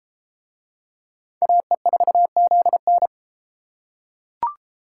Morse code, heard as a keyed tone, about 32 wpm, 700 Hz. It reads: AE4ZN